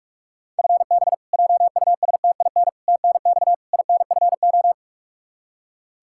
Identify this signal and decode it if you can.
Morse code, heard as a keyed tone, about 46 wpm, 700 Hz. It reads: FB JUSTIN TNX INFO
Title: Practice QSO - Exchange 3 - More Info - 45wpm